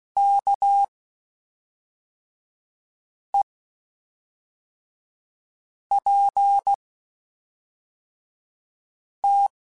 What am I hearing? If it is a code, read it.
Morse code, heard as a keyed tone, about 16 wpm, spaced as for 3 wpm, 800 Hz. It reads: KEPT
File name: Quotes Spaced / Quotes Spaced-mp3s / Character sp @5